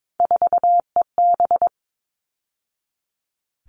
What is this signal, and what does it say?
Morse code, heard as a keyed tone, about 22 wpm, 700 Hz. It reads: 4EB
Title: All Cut Numbers - 3-Digit 22wpm